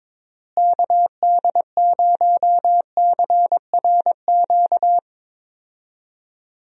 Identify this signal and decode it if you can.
Morse code, heard as a keyed tone, about 22 wpm, 700 Hz. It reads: KD0CRQ
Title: USA Call Signs - 6 Characters - Repeated 3x - 22wpm